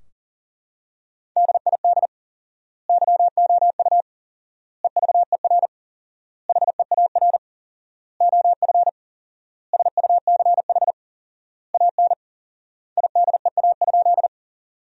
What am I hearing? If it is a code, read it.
Morse code, heard as a keyed tone, about 40 wpm, 700 Hz. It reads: DID YOU EVER HEAR OF SUCH AN IDEA?
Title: 4x Word-Spacing - Sentences from Top 400 Words - 40wpm